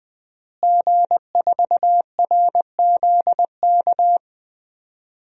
Morse code, heard as a keyed tone, about 21 wpm, 700 Hz. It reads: G4RZK
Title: DXCC-100 Call Signs - 5 Characters - Repeated 3x - 20wpm